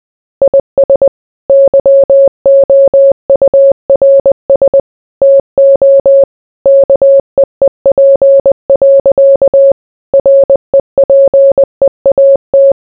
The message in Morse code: IS YOURS TO KEEP. REPEAT